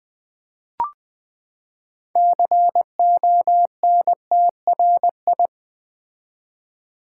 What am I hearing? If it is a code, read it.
Morse code, heard as a keyed tone, about 20 wpm, 700 Hz. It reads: CONTRI
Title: Build A Word - 6 Letters At A Time - 12-Letter Words - 20wpm